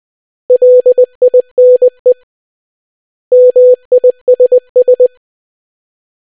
Morse code, heard as a keyed tone, about 20 wpm, 500 Hz. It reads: LINE MISS